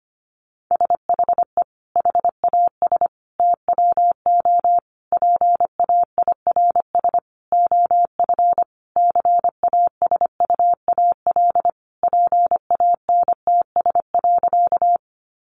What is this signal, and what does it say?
Morse code, heard as a keyed tone, about 25 wpm, 700 Hz. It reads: SHE HAS TWO PAIRS OF CASUAL PANTS.
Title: Additional Easy Phrases 25wpm